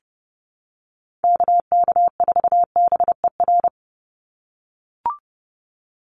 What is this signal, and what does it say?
Morse code, heard as a keyed tone, about 30 wpm, 700 Hz. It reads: KK4BER
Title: USA Call Signs 30wpm